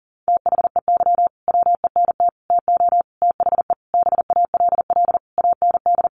Morse code, heard as a keyed tone, about 40 wpm, 700 Hz. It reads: THEY WENT TO THE BALL AND
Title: Sentences from Top 900 Words 40wpm